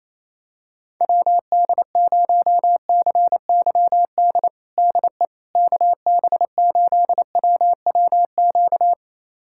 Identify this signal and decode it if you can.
Morse code, heard as a keyed tone, about 28 wpm, 700 Hz. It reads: WD0CYD DE KB8WWQ
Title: USA Callsign DE Callsign - 6 Characters - 28wpm